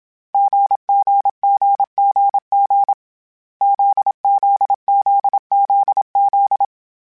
Morse code, heard as a keyed tone, about 27 wpm, 800 Hz. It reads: GGGGG ZZZZZ
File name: Difficult Combos / Difficult Combos-mp3s / GZ GZ7 @25